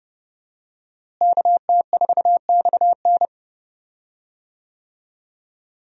Morse code, heard as a keyed tone, about 29 wpm, 700 Hz. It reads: KT4XN
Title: USA Call Signs - 5 Characters - Repeated 3x - 30wpm